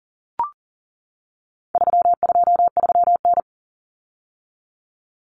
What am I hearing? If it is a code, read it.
Morse code, heard as a keyed tone, about 40 wpm, 700 Hz. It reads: 323N
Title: Common Cut Numbers - 4-Digit 40wpm